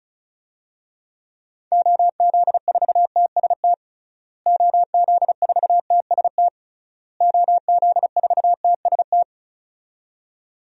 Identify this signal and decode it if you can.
Morse code, heard as a keyed tone, about 34 wpm, 700 Hz. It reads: OZ4TST OZ4TST OZ4TST
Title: DXCC-100 Call Signs - 6 Characters - Repeated 3x - 35wpm